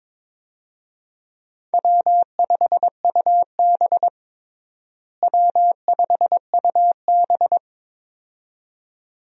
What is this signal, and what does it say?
Morse code, heard as a keyed tone, about 22 wpm, 700 Hz. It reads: W5UB W5UB
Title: USA Call Signs - 4 Characters - Repeated 2x - 22wpm